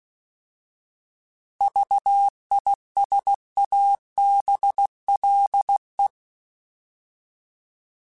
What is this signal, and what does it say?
Morse code, heard as a keyed tone, about 16 wpm, 800 Hz. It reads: VISABLE